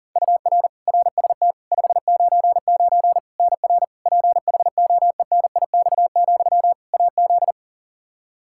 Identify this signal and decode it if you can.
Morse code, heard as a keyed tone, about 40 wpm, 700 Hz. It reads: UR RST 599 NR PHOENIX, AZ
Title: Practice QSO - Exchange 1 - Beginning - 40wpm